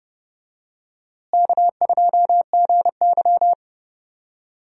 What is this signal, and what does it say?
Morse code, heard as a keyed tone, about 32 wpm, 700 Hz. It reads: K2GY
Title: USA Call Signs - 4 Characters - Repeated 2x - 30wpm